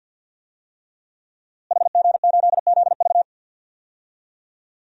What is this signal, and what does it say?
Morse code, heard as a keyed tone, about 52 wpm, 700 Hz. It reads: HG8ZV
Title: DXCC-100 Call Signs - 5 Characters - Repeated 3x - 50wpm